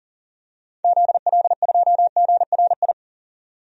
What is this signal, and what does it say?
Morse code, heard as a keyed tone, about 37 wpm, 700 Hz. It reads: ZL2GRI
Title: DXCC-100 Call Signs - 6 Characters - 40wpm